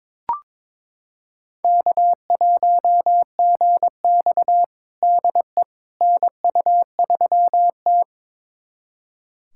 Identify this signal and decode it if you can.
Morse code, heard as a keyed tone, about 22 wpm, 700 Hz. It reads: K1GX DE NU3T
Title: USA Callsign DE Callsign - 4 Characters - 22wpm